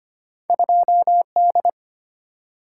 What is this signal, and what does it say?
Morse code, heard as a keyed tone, about 25 wpm, 700 Hz. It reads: 2D
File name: Two Letter-Number Combinations - 25wpm